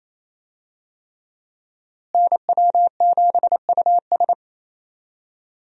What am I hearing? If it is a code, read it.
Morse code, heard as a keyed tone, about 28 wpm, 700 Hz. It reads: NW7US
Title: USA Call Signs - 5 Characters - Repeated 3x - 28wpm